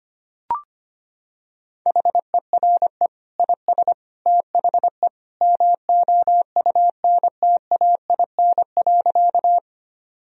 Morse code, heard as a keyed tone, about 25 wpm, 700 Hz. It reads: HERE IS THE MOUNTAIN.
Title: Sentences from Top 1000 Words - Review Part 1 - 25wpm